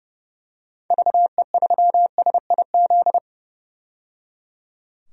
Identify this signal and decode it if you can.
Morse code, heard as a keyed tone, about 31 wpm, 700 Hz. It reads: VE3SIZ
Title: DXCC-100 Call Signs - 6 Characters - 30wpm